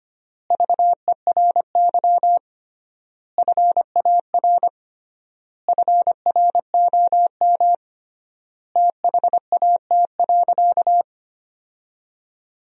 Morse code, heard as a keyed tone, about 25 wpm, 700 Hz. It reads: VERY FAR FROM THAT.
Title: 3x Word-Spacing - Sentences from Top 300 Words - 25wpm